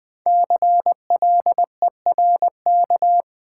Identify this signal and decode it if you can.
Morse code, heard as a keyed tone, about 20 wpm, 700 Hz. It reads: CLERK
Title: Five Letter Words - 20wpm